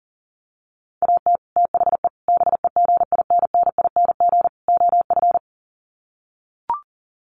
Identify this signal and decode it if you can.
Morse code, heard as a keyed tone, about 40 wpm, 700 Hz. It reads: AT THE BEGINNING OF
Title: Sets of 4 Words 40wpm